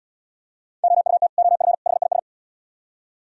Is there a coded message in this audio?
Yes: Morse code from 0.8 to 2.2 s, heard as a keyed tone, about 70 wpm, 700 Hz. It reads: OUT OF HIS